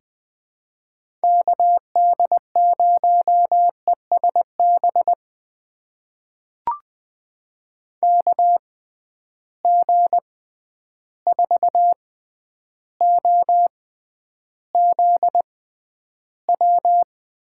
Morse code, heard as a keyed tone, about 20 wpm, spaced as for 10 wpm, 700 Hz. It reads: KD0ESB KG4OZW
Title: USA Call Signs - 6 Characters - 6x Character Spacing - 20wpm